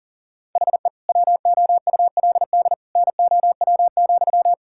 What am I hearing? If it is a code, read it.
Morse code, heard as a keyed tone, about 37 wpm, 700 Hz. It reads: HE WOULD NOW,
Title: Sentences from Top 1000 Words - Review Part 5 - 40wpm